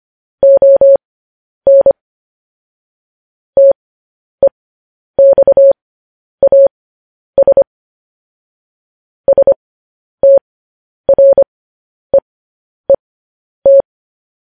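Morse code, heard as a keyed tone, about 25 wpm, spaced as for 10 wpm, 550 Hz. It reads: ON TEXAS STREET
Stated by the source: Morse code news headlines broadcast.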